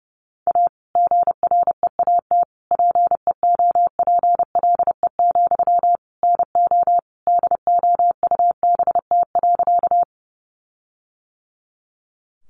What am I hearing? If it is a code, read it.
Morse code, heard as a keyed tone, about 30 wpm, 700 Hz. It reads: A GREAT PEOPLE, NO DOUBT.
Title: Sentences from Top 1500 Words 30wpm